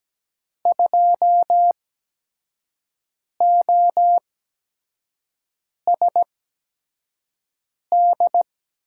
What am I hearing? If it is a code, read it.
Morse code, heard as a keyed tone, about 17 wpm, spaced as for 5 wpm, 700 Hz. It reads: 2OSD
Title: USA Call Signs - 5 Characters - 8x Character Spacing - 17wpm